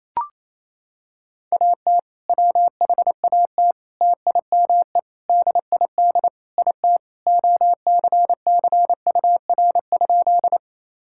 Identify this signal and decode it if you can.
Morse code, heard as a keyed tone, about 28 wpm, 700 Hz. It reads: AT WHAT TIME DID IT OCCUR?
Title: Sentences from Top 1000 Words - Review Part 6 - 28wpm